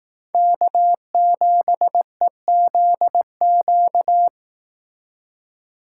Morse code, heard as a keyed tone, about 18 wpm, 700 Hz. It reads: K7EZQ
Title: Speed Racing - USA 1x3 Call Signs 15wpm